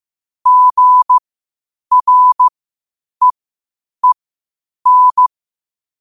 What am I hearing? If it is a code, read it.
Morse code, heard as a keyed tone, about 15 wpm, spaced as for 8 wpm, 1000 Hz. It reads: GREEN
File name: qotdfiles / 8wpm